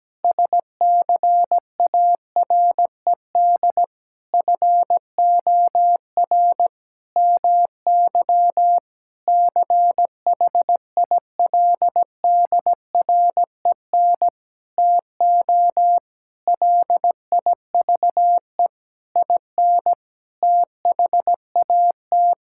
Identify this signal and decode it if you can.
Morse code, heard as a keyed tone, about 17 wpm, 700 Hz. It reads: SCARED FOR MY CHILDREN TO LIVE IN THAT
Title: Sentences from Top 1000 Words - Review Part 5 - 17wpm